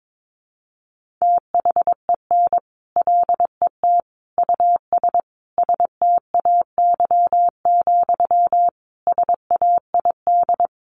Morse code, heard as a keyed tone, about 22 wpm, 700 Hz. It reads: THEN LET US STAY, SAID
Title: Sentences from Top 1500 Words 22wpm